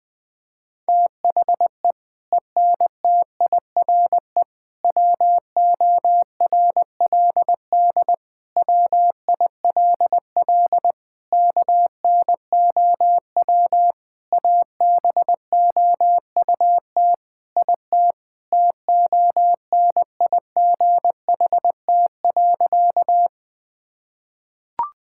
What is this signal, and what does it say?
Morse code, heard as a keyed tone, about 20 wpm, 700 Hz. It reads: THE ENTIRE WORLD WILL KNOW ABOUT IT TONIGHT.
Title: Sentences from Top 1000 Words - Review Part 5 - 20wpm